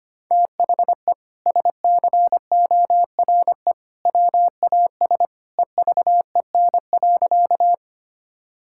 Morse code, heard as a keyed tone, about 25 wpm, 700 Hz. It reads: THE SCORE WAS EVEN.
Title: Sentences from Top 1600 Words 25wpm